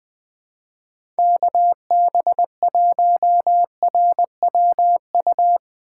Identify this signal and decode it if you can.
Morse code, heard as a keyed tone, about 20 wpm, 700 Hz. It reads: KB1RWU
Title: USA Call Signs - 6 Characters - 3x Character Spacing - 20wpm